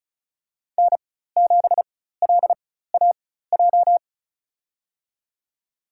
Morse code, heard as a keyed tone, about 34 wpm, spaced as for 16 wpm, 700 Hz. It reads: N7LAJ
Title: USA Call Signs - 5 Characters - 4x Character Spacing - 35wpm